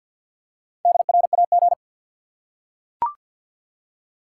Morse code, heard as a keyed tone, about 52 wpm, 700 Hz. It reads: DRAG
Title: Four Letter Words - 50wpm